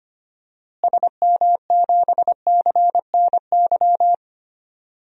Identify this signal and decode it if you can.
Morse code, heard as a keyed tone, about 25 wpm, 700 Hz. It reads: SM7CNY
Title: DXCC-100 Call Signs - 6 Characters - Repeated 3x - 25wpm